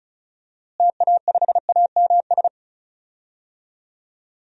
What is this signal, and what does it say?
Morse code, heard as a keyed tone, about 34 wpm, 700 Hz. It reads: TA5AMS
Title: DXCC-100 Call Signs - 6 Characters - Repeated 3x - 35wpm